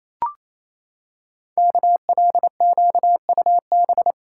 Morse code, heard as a keyed tone, about 29 wpm, 700 Hz. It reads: KLQUB
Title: Code Groups - 5 Letter:Numbers - 28wpm